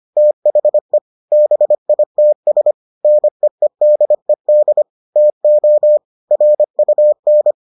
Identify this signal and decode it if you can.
Morse code, heard as a keyed tone, about 25 wpm, 600 Hz. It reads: THE BITS NEEDED TO RUN